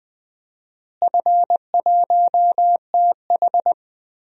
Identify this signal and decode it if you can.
Morse code, heard as a keyed tone, about 20 wpm, 700 Hz. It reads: F1TH
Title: DXCC-100 Call Signs - 4 Characters - Repeated 3x - 20wpm